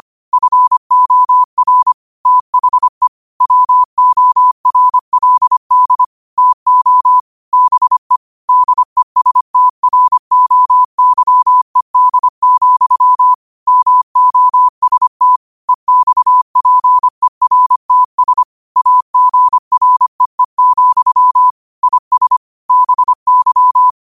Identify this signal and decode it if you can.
Morse code, heard as a keyed tone, about 25 wpm, 1000 Hz. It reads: FOR THE WORLD TO BE DESTROYED, MOST EXPERTS AGREE, IS BY
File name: qotdfiles / 25wpm